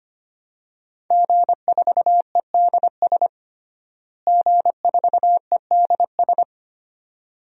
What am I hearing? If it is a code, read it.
Morse code, heard as a keyed tone, about 25 wpm, 700 Hz. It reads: G4EDS G4EDS
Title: DXCC-100 Call Signs - 5 Characters - Repeated 2x - 25wpm